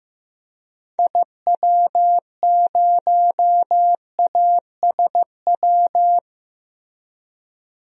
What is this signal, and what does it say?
Morse code, heard as a keyed tone, about 15 wpm, 700 Hz. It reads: IW0ASW
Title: DXCC-100 Call Signs - 6 Characters - Repeated 2x - 15wpm